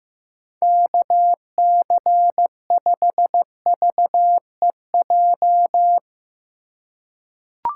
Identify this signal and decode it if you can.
Morse code, heard as a keyed tone, about 15 wpm, 700 Hz. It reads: KC5VEJ
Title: USA Call Signs - 4x Character Spacing - 15wpm